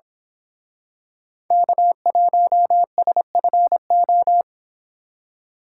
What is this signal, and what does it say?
Morse code, heard as a keyed tone, about 26 wpm, 700 Hz. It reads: K1SFO
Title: Speed Racing - USA Call Signs 22wpm